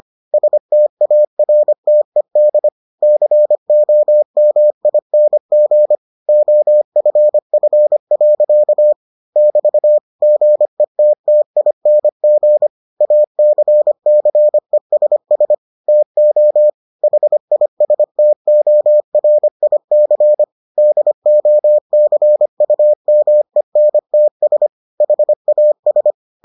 Morse code, heard as a keyed tone, about 25 wpm, 600 Hz. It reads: STARTED COMING OFF. <BT> GETTING ACCESS TO HISTORIC DOCUMENTS HAS